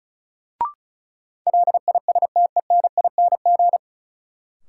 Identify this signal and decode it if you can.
Morse code, heard as a keyed tone, about 36 wpm, 700 Hz. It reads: LISTENING
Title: Sets of 5 Words - New Word Drill - 35wpm